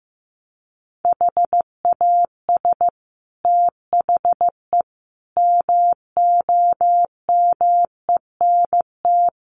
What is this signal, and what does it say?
Morse code, heard as a keyed tone, about 14 wpm, 700 Hz. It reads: HAS THE MOMENT
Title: Sentences from Top 1000 Words 15wpm